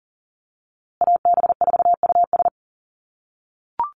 Morse code, heard as a keyed tone, about 40 wpm, 700 Hz. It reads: AB4US
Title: USA Call Signs Repeated 2x 40wpm